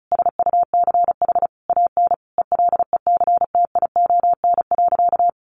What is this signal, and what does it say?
Morse code, heard as a keyed tone, about 34 wpm, 700 Hz. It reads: SUCH AN ELECTION.
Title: Sentences from Top 1500 Words 35wpm